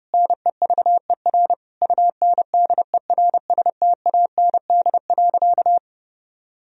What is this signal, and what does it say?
Morse code, heard as a keyed tone, about 30 wpm, 700 Hz. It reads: NEVER UNDERSTAND.
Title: Sentences from Top 1000 Words - Review Part 3 - 30wpm